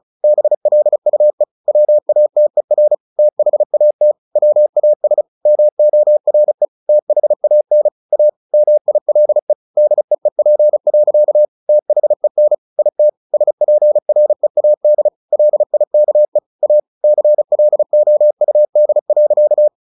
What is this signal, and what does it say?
Morse code, heard as a keyed tone, about 34 wpm, 600 Hz. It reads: BLUE WATER THAT WAS MORE THAN A MILE DEEP. THEN IT SPREAD LIKE A CLOUD.
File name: the old man and the sea 2 35wpm